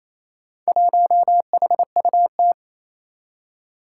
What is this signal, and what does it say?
Morse code, heard as a keyed tone, about 27 wpm, 700 Hz. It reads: 1HUT
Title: Four Challenging Letter-Number Combinations 28wpm